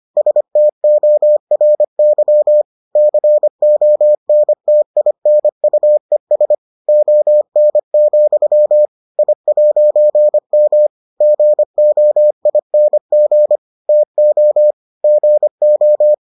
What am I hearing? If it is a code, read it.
Morse code, heard as a keyed tone, about 25 wpm, 600 Hz. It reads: STORY CONTINUES ON, I'M GOING TO GO